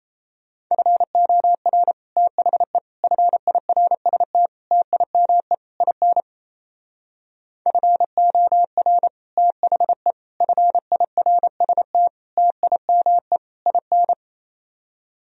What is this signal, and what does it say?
Morse code, heard as a keyed tone, about 37 wpm, 700 Hz. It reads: FOR THE FIRST TIME IN FOR THE FIRST TIME IN